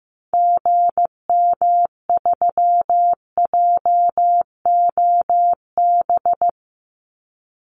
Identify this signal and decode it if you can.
Morse code, heard as a keyed tone, about 15 wpm, 700 Hz. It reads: GM3JOB
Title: Call Signs Repeated 2x - 15wpm